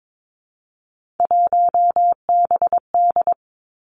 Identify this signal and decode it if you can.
Morse code, heard as a keyed tone, about 22 wpm, 700 Hz. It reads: 1BD
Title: Three Challenging Letter-Number Combinations 22wpm